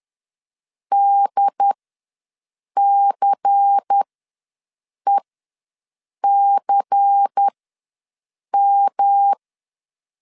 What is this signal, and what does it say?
Morse code, heard as a keyed tone, about 10 wpm, spaced as for 6 wpm, 800 Hz. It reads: DCECM
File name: LEZIONI / 5 LBVC / 5 LBVC ws 10wpm 3733